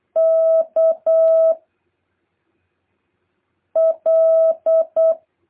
Morse code, heard as a keyed tone, about 8 wpm, 650 Hz. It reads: K L
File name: jh8cbh / 2023cw